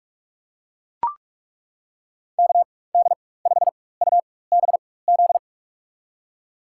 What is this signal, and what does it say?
Morse code, heard as a keyed tone, about 41 wpm, spaced as for 20 wpm, 700 Hz. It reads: KD5UBZ
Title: USA Call Signs - 6 Characters - 4x Character Spacing - 45wpm